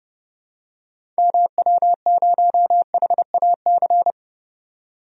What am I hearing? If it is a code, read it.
Morse code, heard as a keyed tone, about 31 wpm, 700 Hz. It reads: MW0HAC
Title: DXCC-100 Call Signs - 6 Characters - 30wpm